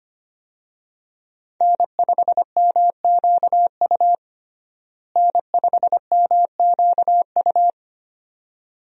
Morse code, heard as a keyed tone, about 25 wpm, 700 Hz. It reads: N5MQU N5MQU
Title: USA Call Signs - 5 Characters - Repeated 2x - 25wpm